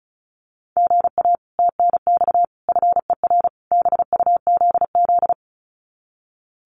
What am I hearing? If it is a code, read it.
Morse code, heard as a keyed tone, about 34 wpm, 700 Hz. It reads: GA TNX FER BUZZ
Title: Common QSO Phrases Encore 35wpm